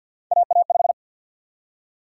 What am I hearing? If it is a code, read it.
Morse code, heard as a keyed tone, about 52 wpm, 700 Hz. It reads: AA5
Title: USA Call Sign Prefixes - 50wpm